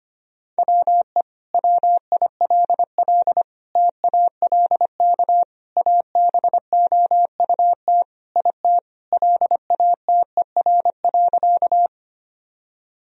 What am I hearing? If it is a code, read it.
Morse code, heard as a keyed tone, about 25 wpm, 700 Hz. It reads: WE WILL TALK ABOUT IT LATER.
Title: Sentences from Top 1000 Words - Review Part 4 - 25wpm